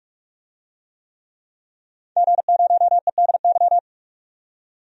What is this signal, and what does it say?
Morse code, heard as a keyed tone, about 42 wpm, 700 Hz. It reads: G0EDY
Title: DXCC-100 Call Signs - 5 Characters - 45wpm